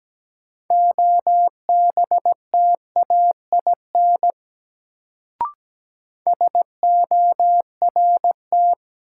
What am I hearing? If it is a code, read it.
Morse code, heard as a keyed tone, about 17 wpm, 700 Hz. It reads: OBTAIN SORT
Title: Sentences from Top 1300 Words - New Word Drill Alt - 17wpm